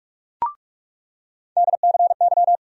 Morse code, heard as a keyed tone, about 46 wpm, 700 Hz. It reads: DCY
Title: USA Call Sign Postfixes - 3 Letters 45wpm